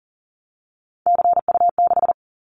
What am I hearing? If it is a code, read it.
Morse code, heard as a keyed tone, about 42 wpm, 700 Hz. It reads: CU6
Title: Callsign Prefixes for Top 50 DXCC - 40wpm